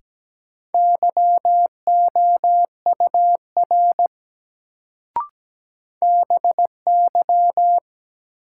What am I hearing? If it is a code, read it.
Morse code, heard as a keyed tone, about 17 wpm, 700 Hz. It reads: YOUR BY
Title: Sentences from Top 100 Words - New Word Drill - 17wpm